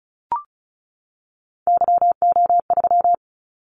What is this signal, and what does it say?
Morse code, heard as a keyed tone, about 36 wpm, 700 Hz. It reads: YO3
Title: Callsign Prefixes for Top 25 DXCC - 35wpm